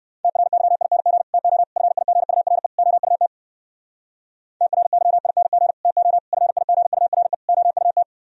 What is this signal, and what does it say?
Morse code, heard as a keyed tone, about 58 wpm, 700 Hz. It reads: TRYING TO FIGURE OUT TRYING TO FIGURE OUT